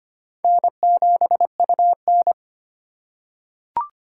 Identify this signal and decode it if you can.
Morse code, heard as a keyed tone, about 25 wpm, 700 Hz. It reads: N7UN
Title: USA Call Signs - 4 Characters - 2x Character Spacing - 25wpm